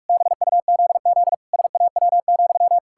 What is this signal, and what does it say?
Morse code, heard as a keyed tone, about 46 wpm, 700 Hz. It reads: BUZZ SAW,
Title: Pangram Challenge v2 - 45wpm